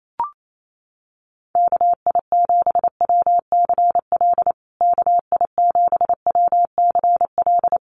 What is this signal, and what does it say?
Morse code, heard as a keyed tone, about 28 wpm, 700 Hz. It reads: KI7WCL KI7WCL
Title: USA Call Signs Repeated 3x 28wpm